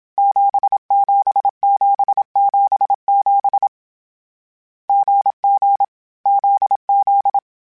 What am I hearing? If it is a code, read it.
Morse code, heard as a keyed tone, about 27 wpm, 800 Hz. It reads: 77777 GG ZZ
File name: Short Runs / Short Runs-mp3s / GZ GZ7 @25